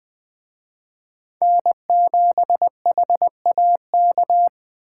Morse code, heard as a keyed tone, about 20 wpm, 700 Hz. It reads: N7HAK
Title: Speed Racing - USA 1x3 Call Signs 17wpm